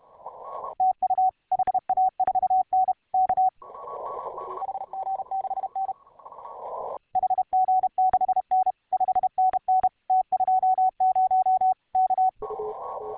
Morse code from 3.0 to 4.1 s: K